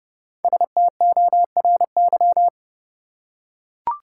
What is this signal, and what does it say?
Morse code, heard as a keyed tone, about 30 wpm, 700 Hz. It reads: STORY
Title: Five Letter Words - 30wpm